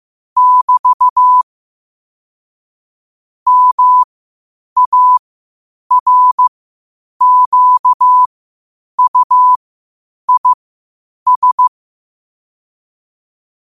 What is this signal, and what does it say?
Morse code, heard as a keyed tone, about 15 wpm, spaced as for 8 wpm, 1000 Hz. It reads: <BT> MARQUIS
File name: qotdfiles / 8wpm